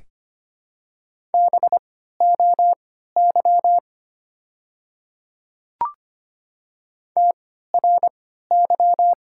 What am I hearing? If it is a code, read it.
Morse code, heard as a keyed tone, about 25 wpm, spaced as for 14 wpm, 700 Hz. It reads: BOY TRY
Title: 3x Character Spacing - Top 300 Words - 25wpm